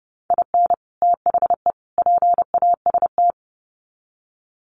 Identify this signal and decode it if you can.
Morse code, heard as a keyed tone, about 30 wpm, 700 Hz. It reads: IN THE PAST
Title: Sets of 3 Words 30wpm